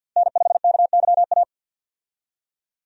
Morse code, heard as a keyed tone, about 51 wpm, 700 Hz. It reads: N5KYA
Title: USA Call Signs - 5 Characters - Repeated 2x - 50wpm